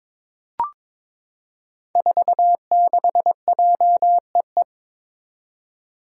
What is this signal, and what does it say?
Morse code, heard as a keyed tone, about 22 wpm, 700 Hz. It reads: 46JEE
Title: Five Challenging Letter-Number Combinations 22wpm